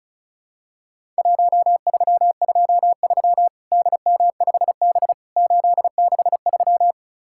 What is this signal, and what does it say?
Morse code, heard as a keyed tone, about 36 wpm, 700 Hz. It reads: 1323 DM5B 863
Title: Worked All Europe DX Contest - QTC Exchange - 35wpm